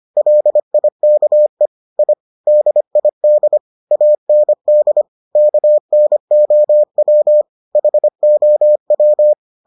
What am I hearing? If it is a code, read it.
Morse code, heard as a keyed tone, about 25 wpm, 600 Hz. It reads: LIKE I DID AND KNOW HOW